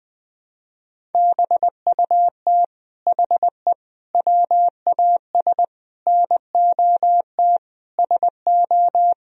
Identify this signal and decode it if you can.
Morse code, heard as a keyed tone, about 21 wpm, 700 Hz. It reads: BUT HE WAS NOT SO
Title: Sentences from Top 1000 Words - Review Part 3 - 20wpm